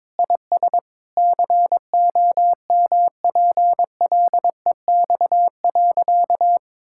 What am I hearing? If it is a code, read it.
Morse code, heard as a keyed tone, about 21 wpm, 700 Hz. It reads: IS COMPLEX.
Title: Sentences from Top 1000 Words - Review Part 6 - 22wpm